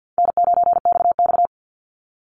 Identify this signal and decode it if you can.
Morse code, heard as a keyed tone, about 50 wpm, 700 Hz. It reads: N9XX
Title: Easy Call Signs Repeated 2x - 50wpm